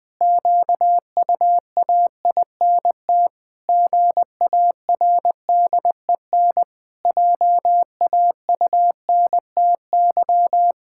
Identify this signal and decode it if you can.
Morse code, heard as a keyed tone, about 20 wpm, 700 Hz. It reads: QUAINT GARDEN JAUNTY
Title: Pangram Challenge v2 - 20wpm